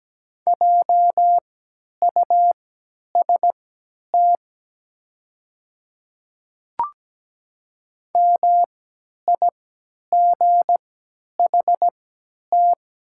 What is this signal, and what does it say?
Morse code, heard as a keyed tone, about 17 wpm, spaced as for 10 wpm, 700 Hz. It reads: JUST MIGHT
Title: 3x Character Spacing - Top 500 Words - 17wpm